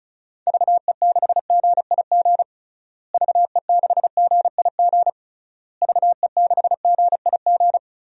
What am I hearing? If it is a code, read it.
Morse code, heard as a keyed tone, about 34 wpm, 700 Hz. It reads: VE6GIG VE6GIG VE6GIG
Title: DXCC-100 Call Signs - 6 Characters - Repeated 3x - 35wpm